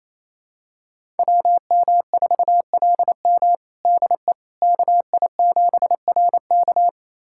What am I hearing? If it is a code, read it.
Morse code, heard as a keyed tone, about 28 wpm, 700 Hz. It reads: WM4LM DE KI7RK
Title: USA Callsign DE Callsign - 5 Characters - 28wpm